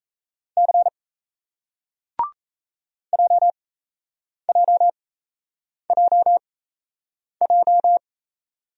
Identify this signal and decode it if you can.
Morse code, heard as a keyed tone, about 38 wpm, spaced as for 8 wpm, 700 Hz. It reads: C JJJJ